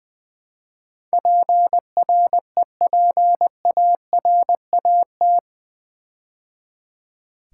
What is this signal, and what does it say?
Morse code, heard as a keyed tone, about 20 wpm, 700 Hz. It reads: PREPARAT
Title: Build A Word - 4 Letters At A Time - 11-Letter Words - 20wpm